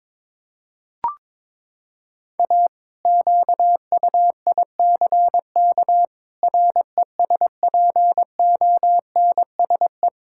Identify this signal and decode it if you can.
Morse code, heard as a keyed tone, about 22 wpm, 700 Hz. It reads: A QUICK RESPONSE
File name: practice / 22wpm / Sentences with 8-Word Phrases - 22wpm